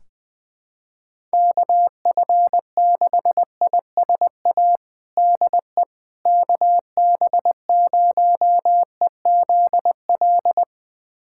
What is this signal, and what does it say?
Morse code, heard as a keyed tone, about 20 wpm, 700 Hz. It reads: KF6ISA DE KB0EZL
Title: USA Callsign DE Callsign - 6 Characters - 20wpm